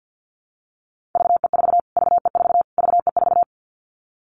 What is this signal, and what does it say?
Morse code, heard as a keyed tone, about 48 wpm, 700 Hz. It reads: VE4 VE4 VE4
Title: Callsign Prefixes for Top 25 DXCC - Repeated 3x - 50wpm